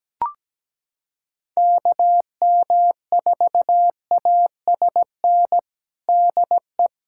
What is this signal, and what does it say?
Morse code, heard as a keyed tone, about 17 wpm, 700 Hz. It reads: KM4ASN DE
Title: USA Callsign DE Callsign - 6 Characters - 17wpm